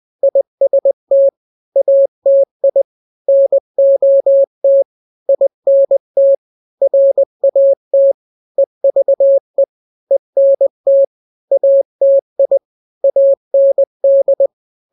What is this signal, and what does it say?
Morse code, heard as a keyed tone, about 20 wpm, 550 Hz. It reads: IST ATI NOT INT RAT EVE ENT ATI AND
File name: n-grams / 3-grams 20 18 WPM